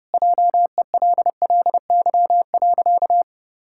Morse code, heard as a keyed tone, about 30 wpm, 700 Hz. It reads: JELLY.
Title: Pangram Challenge v2 - 30wpm